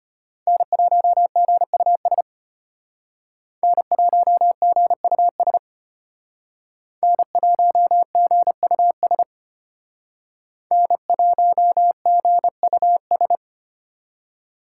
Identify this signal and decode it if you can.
Morse code, heard as a keyed tone, about 33 wpm, 700 Hz. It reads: N1GUS N1GUS N1GUS N1GUS